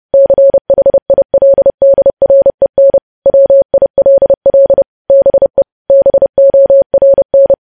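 Morse code, heard as a keyed tone, about 30 wpm, 550 Hz. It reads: CHILDREN WILL BE BORN
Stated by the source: News headlines sent in Morse at 30 words per minute.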